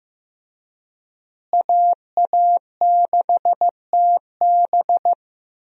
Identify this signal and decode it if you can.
Morse code, heard as a keyed tone, about 15 wpm, 700 Hz. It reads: AA6TB
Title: USA Call Signs - 5 Characters - Repeated 2x - 15wpm